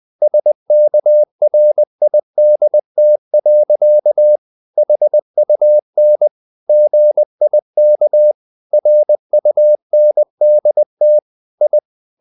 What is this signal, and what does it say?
Morse code, heard as a keyed tone, about 20 wpm, 600 Hz. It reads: SKRIDT. HUN GIK RUNDT I
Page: Ferie 19 wpm